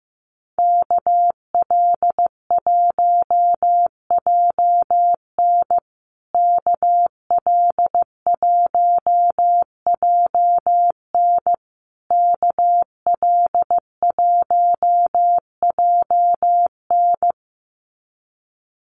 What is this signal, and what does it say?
Morse code, heard as a keyed tone, about 15 wpm, 700 Hz. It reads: KL1JN KL1JN KL1JN
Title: USA Call Signs Repeated 3x 15wpm